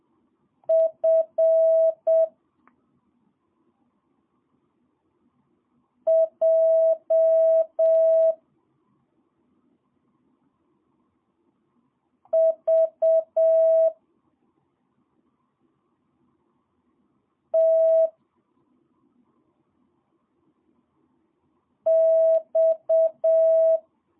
Morse code, heard as a keyed tone, about 7 wpm, 650 Hz. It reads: F J V T X
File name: jh8cbh / 2022cw